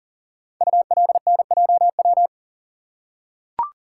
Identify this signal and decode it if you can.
Morse code, heard as a keyed tone, about 39 wpm, 700 Hz. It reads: ULNJW
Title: Code Groups - 5 Letters - 40wpm